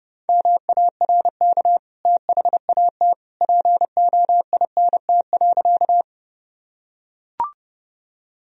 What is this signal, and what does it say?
Morse code, heard as a keyed tone, about 30 wpm, 700 Hz. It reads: MARK THAT POINT.
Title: Sentences from Top 1000 Words - Review Part 1 - 30wpm